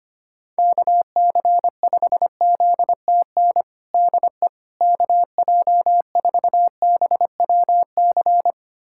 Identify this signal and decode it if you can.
Morse code, heard as a keyed tone, about 25 wpm, 700 Hz. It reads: KC5ZTN DE KJ4BWC
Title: USA Callsign DE Callsign 25wpm